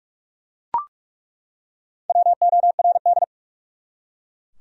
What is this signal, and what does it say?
Morse code, heard as a keyed tone, about 42 wpm, 700 Hz. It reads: WORD
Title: Top 100 Words 45wpm